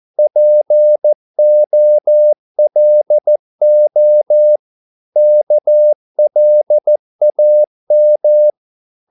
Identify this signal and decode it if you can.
Morse code, heard as a keyed tone, about 14 wpm, 600 Hz. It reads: POLO KLAM
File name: Korte ord DK 14wpm